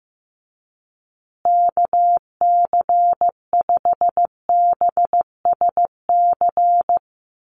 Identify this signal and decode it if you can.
Morse code, heard as a keyed tone, about 15 wpm, 700 Hz. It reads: KC5BSC
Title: USA Call Signs Repeated 3x 15wpm